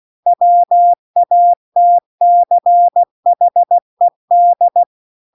Morse code, heard as a keyed tone, about 16 wpm, 700 Hz. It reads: WATCHED